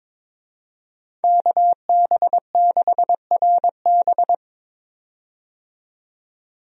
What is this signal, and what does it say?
Morse code, heard as a keyed tone, about 22 wpm, 700 Hz. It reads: KB6RB
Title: USA Call Signs - 5 Characters - 22wpm